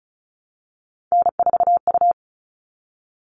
Morse code, heard as a keyed tone, about 34 wpm, 700 Hz. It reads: N4U
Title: All Cut Numbers - 3-Digit 35wpm